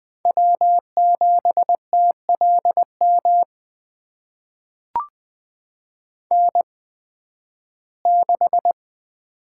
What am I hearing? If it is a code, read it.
Morse code, heard as a keyed tone, about 20 wpm, spaced as for 13 wpm, 700 Hz. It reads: W7TLM N6
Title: USA Call Signs - 8x Character Spacing - 20wpm